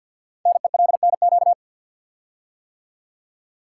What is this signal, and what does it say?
Morse code, heard as a keyed tone, about 48 wpm, 700 Hz. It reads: NELNQ
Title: Code Groups - 5 Letters - 50wpm